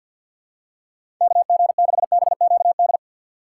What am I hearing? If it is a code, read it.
Morse code, heard as a keyed tone, about 52 wpm, 700 Hz. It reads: KG6BQD
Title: USA Call Signs - 4x Character Spacing - 50wpm